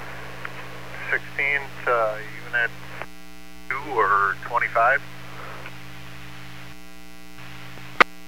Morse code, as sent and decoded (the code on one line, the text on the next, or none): none